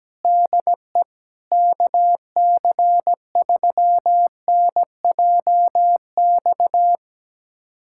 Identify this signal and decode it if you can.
Morse code, heard as a keyed tone, about 17 wpm, 700 Hz. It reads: DE KC3NJX
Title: USA Callsign DE Callsign - 6 Characters - 17wpm